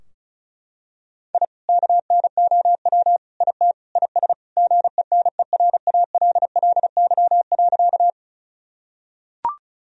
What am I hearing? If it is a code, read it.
Morse code, heard as a keyed tone, about 36 wpm, 700 Hz. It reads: I KNOW IT IS GENERALLY.
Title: Sentences from Top 1000 Words - Review Part 4 - 35wpm